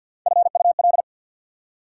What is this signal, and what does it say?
Morse code, heard as a keyed tone, about 54 wpm, 690 Hz. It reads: FUL
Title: Prefixes and Suffixes 50wpm